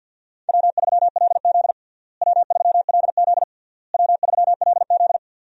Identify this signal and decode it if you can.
Morse code, heard as a keyed tone, about 52 wpm, 700 Hz. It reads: W3LZ W3LZ W3LZ
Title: USA Call Signs - 4 Characters - Repeated 3x - 50wpm